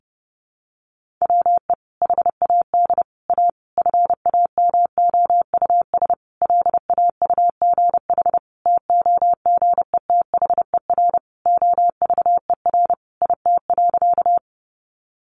Morse code, heard as a keyed tone, about 30 wpm, 700 Hz. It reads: WE HAD A FAMOUS LAUGH TOGETHER OVER IT.
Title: Sentences from Top 1500 Words 30wpm